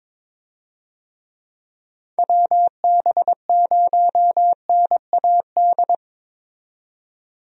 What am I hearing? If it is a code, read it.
Morse code, heard as a keyed tone, about 22 wpm, 700 Hz. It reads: WB0NAD
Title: USA Call Signs - 6 Characters - 22wpm